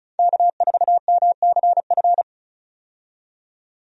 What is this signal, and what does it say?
Morse code, heard as a keyed tone, about 36 wpm, 700 Hz. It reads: K4MCF
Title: USA Call Signs - 5 Characters - 35wpm